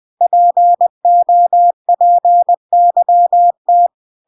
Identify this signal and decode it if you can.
Morse code, heard as a keyed tone, about 20 wpm, 700 Hz. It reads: POPYT